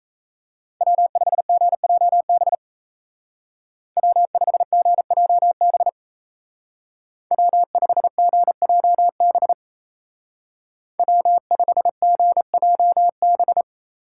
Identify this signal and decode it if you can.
Morse code, heard as a keyed tone, about 35 wpm, 700 Hz. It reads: W5GJB W5GJB W5GJB W5GJB
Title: Speed Racing - USA 1x3 Call Signs 28wpm